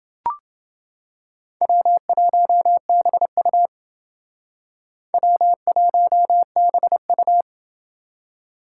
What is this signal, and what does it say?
Morse code, heard as a keyed tone, about 28 wpm, 690 Hz. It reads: W1BU W1BU